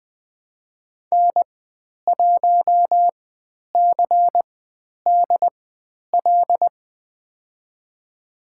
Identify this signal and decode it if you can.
Morse code, heard as a keyed tone, about 20 wpm, spaced as for 10 wpm, 700 Hz. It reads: N1CDL
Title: 095 Lesson 16g - Letter c - Call Signs 20wpm